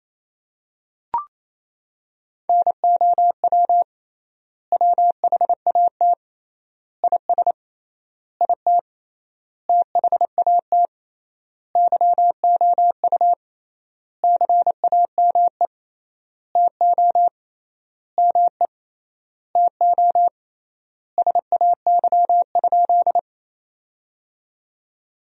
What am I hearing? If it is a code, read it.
Morse code, heard as a keyed tone, about 28 wpm, 700 Hz. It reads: NOW WHAT IS IT THAT YOU CAME TO ME TO SAY?
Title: 3x Word-Spacing - Sentences from Top 200 Words - 28wpm